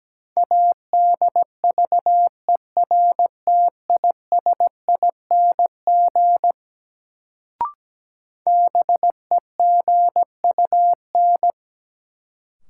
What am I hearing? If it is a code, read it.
Morse code, heard as a keyed tone, about 17 wpm, 700 Hz. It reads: ADVERTISING BEGUN
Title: Sentences from Top 1000 Words - Review Part 6 - New Word Drill - 17wpm